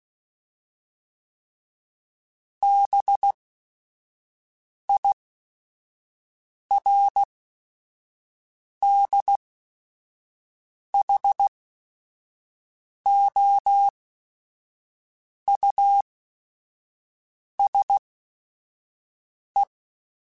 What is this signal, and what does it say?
Morse code, heard as a keyed tone, about 16 wpm, spaced as for 5 wpm, 800 Hz. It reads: BIRDHOUSE
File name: General / General-mp3s / Houses @5